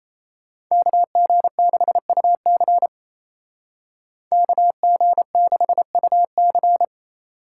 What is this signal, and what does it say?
Morse code, heard as a keyed tone, about 34 wpm, 700 Hz. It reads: KG6UC KG6UC
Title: Speed Racing - USA 2x2 Call Signs 28wpm